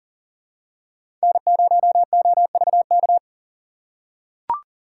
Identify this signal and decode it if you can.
Morse code, heard as a keyed tone, about 40 wpm, 700 Hz. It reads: N0OVK
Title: USA Call Signs - 3x Character Spacing - 40wpm